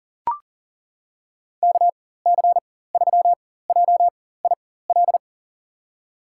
Morse code, heard as a keyed tone, about 40 wpm, spaced as for 19 wpm, 700 Hz. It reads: KC3JIL
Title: USA Call Signs - 6 Characters - 4x Character Spacing - 40wpm